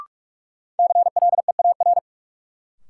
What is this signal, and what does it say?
Morse code, heard as a keyed tone, about 47 wpm, 700 Hz. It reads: CLEAR
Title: Sentences from Top 500 Words - New Word Drill - 45wpm